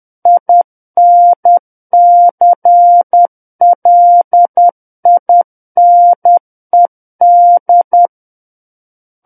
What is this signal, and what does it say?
Morse code, heard as a keyed tone, about 10 wpm, 700 Hz. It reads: INCLINED
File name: Book One / 01-Chapter 1 - 10 WPM